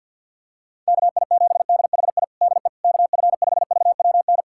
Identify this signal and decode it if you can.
Morse code, heard as a keyed tone, about 46 wpm, 700 Hz. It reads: KI7DHI DE KF5VWN
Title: USA Callsign DE Callsign - 6 Characters - 50wpm